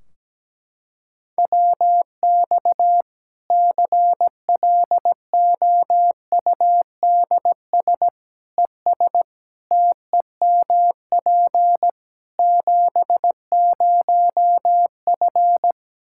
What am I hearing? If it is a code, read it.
Morse code, heard as a keyed tone, about 17 wpm, 700 Hz. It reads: WX CLOUDS ES TEMP 70F
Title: Practice QSO - Exchange 2 - Info - 17wpm